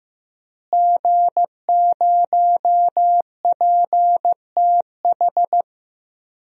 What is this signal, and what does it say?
Morse code, heard as a keyed tone, about 16 wpm, 700 Hz. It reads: G0PTH
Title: DXCC-100 Call Signs - 5 Characters - Repeated 2x - 15wpm